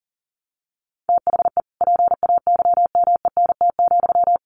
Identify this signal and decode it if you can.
Morse code, heard as a keyed tone, about 40 wpm, 700 Hz. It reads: THE PAYMENT,
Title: Sentences from Top 1200 Words 40wpm